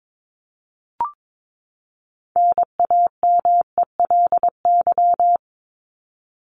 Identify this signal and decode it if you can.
Morse code, heard as a keyed tone, about 22 wpm, 700 Hz. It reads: NAMELY
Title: LY Suffix - Top 100 Words with Base Words 22wpm